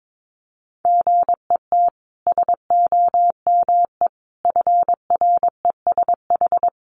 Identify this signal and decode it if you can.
Morse code, heard as a keyed tone, about 22 wpm, 700 Hz. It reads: GET SOME FRESH
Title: Sentences from Top 1200 Words 22wpm